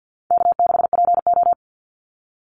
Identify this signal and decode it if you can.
Morse code, heard as a keyed tone, about 50 wpm, 700 Hz. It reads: K6PO
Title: Call Signs Repeated 2x - 50wpm